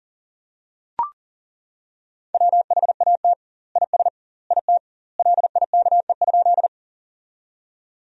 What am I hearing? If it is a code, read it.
Morse code, heard as a keyed tone, about 40 wpm, 700 Hz. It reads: WHAT IS IT LIKE?
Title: 2x Word-Spacing - Sentences from Top 100 Words - 40wpm